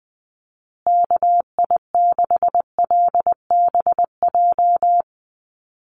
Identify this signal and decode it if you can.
Morse code, heard as a keyed tone, about 20 wpm, 700 Hz. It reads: KI6LBJ
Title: USA Call Signs Repeated 2x 20wpm